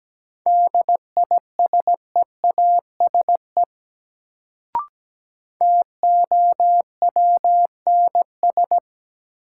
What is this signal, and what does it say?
Morse code, heard as a keyed tone, about 17 wpm, 700 Hz. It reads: DISEASE TOWNS
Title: Sentences from Top 1000 Words - Review Part 6 - New Word Drill - 17wpm